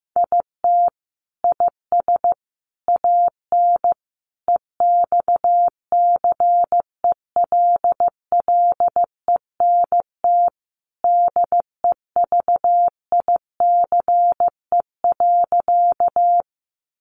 Morse code, heard as a keyed tone, about 15 wpm, 700 Hz. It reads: IT IS AN EXCELLENT DEVICE.
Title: Sentences from Top 1500 Words 15wpm